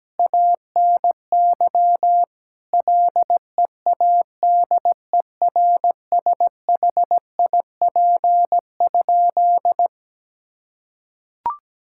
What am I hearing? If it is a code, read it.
Morse code, heard as a keyed tone, about 17 wpm, 700 Hz. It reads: ANY LEADERSHIP?
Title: Sentences from Top 1600 Words 17wpm